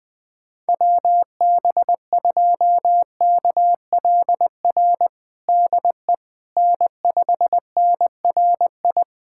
Morse code, heard as a keyed tone, about 20 wpm, 700 Hz. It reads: WB2KLR DE N5NRI
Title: USA Callsign DE Callsign 20wpm